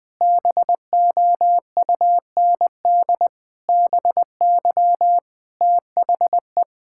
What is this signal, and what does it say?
Morse code, heard as a keyed tone, about 21 wpm, 700 Hz. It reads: BOUND BY THE
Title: Sentences with 9-Word Phrases - 20wpm